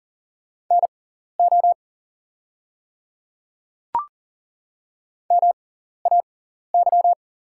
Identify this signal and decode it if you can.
Morse code, heard as a keyed tone, about 40 wpm, spaced as for 14 wpm, 700 Hz. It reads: NO MAY
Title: 6x Character Spacing - Top 100 Words - 40wpm